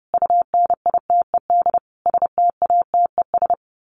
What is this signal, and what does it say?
Morse code, heard as a keyed tone, about 30 wpm, 700 Hz. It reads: UNITED STATES
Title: Sets of 5 Words Encore 30wpm